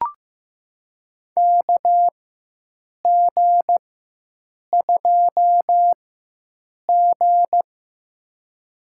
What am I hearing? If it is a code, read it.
Morse code, heard as a keyed tone, about 15 wpm, spaced as for 7 wpm, 700 Hz. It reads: KG2G
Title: USA Call Signs - 4x Character Spacing - 15wpm